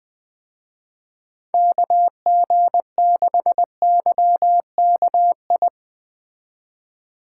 Morse code, heard as a keyed tone, about 20 wpm, 700 Hz. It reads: KG6YKI